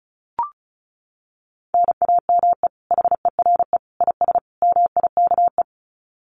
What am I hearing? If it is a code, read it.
Morse code, heard as a keyed tone, about 34 wpm, 700 Hz. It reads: NAME HERE IS MIKE
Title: Common QSO Phrases 35wpm